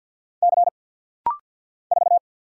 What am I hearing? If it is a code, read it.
Morse code, heard as a keyed tone, about 50 wpm, 690 Hz. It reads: C 4
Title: Single Letter-Number Alt - Rapid-Fire 50wpm